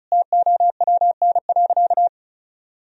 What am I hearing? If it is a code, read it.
Morse code, heard as a keyed tone, about 34 wpm, 700 Hz. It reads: TOWN.
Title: Sentences from Top 1000 Words - Review Part 2 - 35wpm